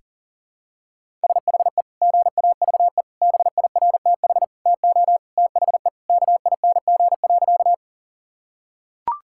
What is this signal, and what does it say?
Morse code, heard as a keyed tone, about 40 wpm, 700 Hz. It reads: SHE GAVE BIRTH TO THE KING.
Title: Sentences from Top 1000 Words - Review Part 5 - 40wpm